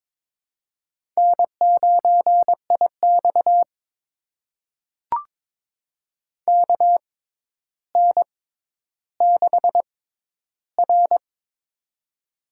Morse code, heard as a keyed tone, about 22 wpm, spaced as for 11 wpm, 700 Hz. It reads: N9IX KN6R
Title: USA Call Signs - 4 Characters - 6x Character Spacing - 22wpm